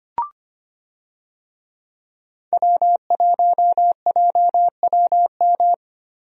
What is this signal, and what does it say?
Morse code, heard as a keyed tone, about 25 wpm, 700 Hz. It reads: W1JWM
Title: USA Call Signs - 5 Characters - 25wpm